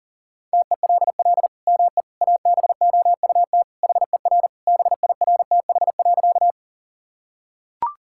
Morse code, heard as a keyed tone, about 37 wpm, 700 Hz. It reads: TELL ME ABOUT HER BIRTH.
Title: Sentences from Top 1000 Words - Review Part 5 - 40wpm